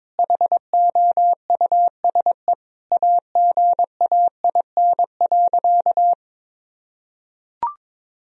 Morse code, heard as a keyed tone, about 22 wpm, 700 Hz. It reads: HOUSE AGAIN.
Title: Sentences from Top 1000 Words - Review Part 3 - 22wpm